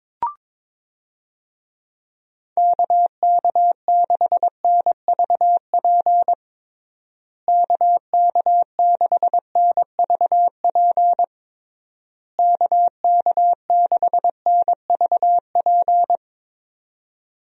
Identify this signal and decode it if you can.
Morse code, heard as a keyed tone, about 22 wpm, 700 Hz. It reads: KK6NVP KK6NVP KK6NVP
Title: USA Call Signs - 6 Characters - Repeated 3x - 22wpm